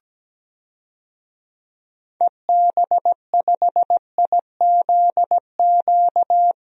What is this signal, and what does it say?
Morse code, heard as a keyed tone, about 17 wpm, 700 Hz. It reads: EB5IZQ
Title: DXCC-100 Call Signs - 6 Characters - Repeated 3x - 17wpm